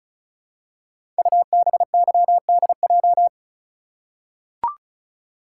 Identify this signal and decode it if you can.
Morse code, heard as a keyed tone, about 36 wpm, 690 Hz. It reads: UBYDJ
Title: Code Groups - 5 Letters - 35wpm